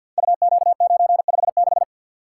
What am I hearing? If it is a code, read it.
Morse code, heard as a keyed tone, about 53 wpm, 700 Hz. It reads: UQ956
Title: Code Groups - 5 Letter:Numbers - 50wpm